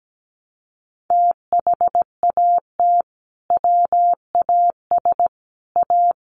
Morse code, heard as a keyed tone, about 17 wpm, 700 Hz. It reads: THAT WAS A
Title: Sentences from Top 800 Words 17wpm